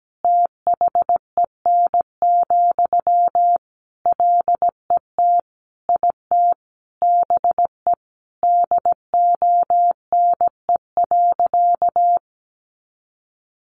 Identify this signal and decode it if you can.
Morse code, heard as a keyed tone, about 17 wpm, 700 Hz. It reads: THEN, LET IT BE DONE.
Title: Sentences from Top 500 Words 17wpm